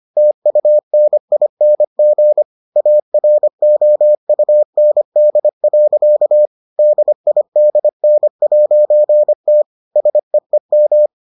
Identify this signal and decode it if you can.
Morse code, heard as a keyed tone, about 25 wpm, 600 Hz. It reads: TUNING AROUND. DIDN'T SEEM